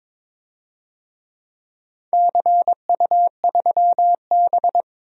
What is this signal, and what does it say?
Morse code, heard as a keyed tone, about 22 wpm, 700 Hz. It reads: CU3B
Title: DXCC-100 Call Signs - 4 Characters - 22wpm